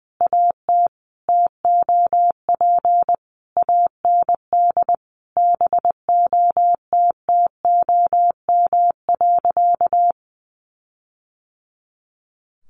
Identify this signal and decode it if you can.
Morse code, heard as a keyed tone, about 20 wpm, 700 Hz. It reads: AT TOP AND BOTTOM.
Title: Sentences from Top 1400 Words 20wpm